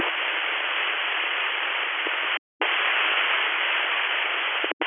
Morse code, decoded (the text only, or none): none